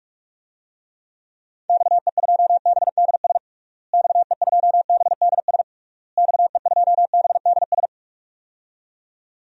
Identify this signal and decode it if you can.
Morse code, heard as a keyed tone, about 42 wpm, 700 Hz. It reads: XE2BDS XE2BDS XE2BDS
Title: DXCC-100 Call Signs - 6 Characters - Repeated 3x - 45wpm